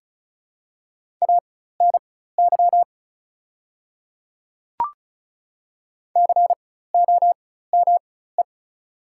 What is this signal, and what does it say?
Morse code, heard as a keyed tone, about 36 wpm, spaced as for 16 wpm, 700 Hz. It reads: ANY COME
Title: 4x Character Spacing - Top 300 Words - 35wpm